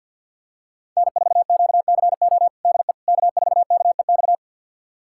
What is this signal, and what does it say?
Morse code, heard as a keyed tone, about 51 wpm, 700 Hz. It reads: N4QCO DE K4KEX
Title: USA Callsign DE Callsign 50wpm